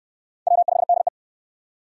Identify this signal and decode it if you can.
Morse code, heard as a keyed tone, about 65 wpm, 700 Hz. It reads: W5DE